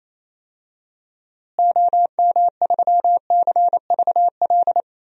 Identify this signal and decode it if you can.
Morse code, heard as a keyed tone, about 28 wpm, 700 Hz. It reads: OM3CVL
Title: DXCC-100 Call Signs - 6 Characters - Repeated 3x - 28wpm